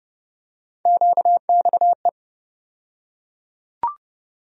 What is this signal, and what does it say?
Morse code, heard as a keyed tone, about 30 wpm, 700 Hz. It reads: QXE
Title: USA Call Sign Postfixes - 3 Letters 30wpm